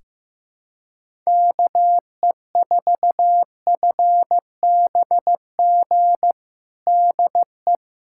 Morse code, heard as a keyed tone, about 15 wpm, 700 Hz. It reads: KE4FBG DE
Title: USA Callsign DE Callsign - 6 Characters - 15wpm